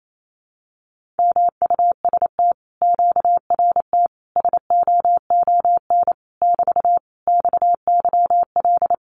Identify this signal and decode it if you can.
Morse code, heard as a keyed tone, about 28 wpm, 700 Hz. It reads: MUST QRT SOON <BT> XYL
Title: Common QSO Phrases Encore 28wpm